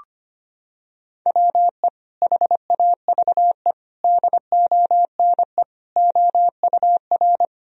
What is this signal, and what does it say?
Morse code, heard as a keyed tone, about 25 wpm, 700 Hz. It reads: WE HAVE DONE OUR
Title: Sentences from Top 1000 Words - Review Part 1 - 25wpm